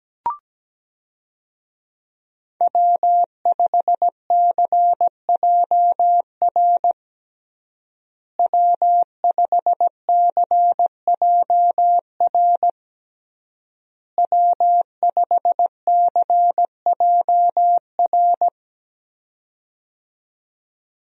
Morse code, heard as a keyed tone, about 17 wpm, 700 Hz. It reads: W5CJR W5CJR W5CJR
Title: USA Call Signs - 5 Characters - Repeated 3x - 17wpm